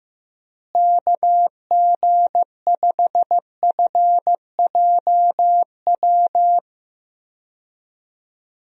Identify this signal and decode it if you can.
Morse code, heard as a keyed tone, about 15 wpm, 700 Hz. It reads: KG5FJW
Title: USA Call Signs - 6 Characters - Repeated 2x - 15wpm